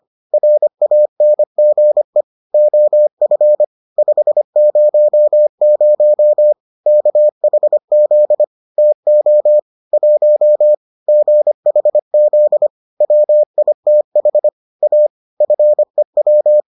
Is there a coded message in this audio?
Yes: Morse code from 0.3 to 16.6 s, heard as a keyed tone, about 25 wpm, 600 Hz. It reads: RANGE OF 500 KHZ TO 1 GHZ WITH A FEW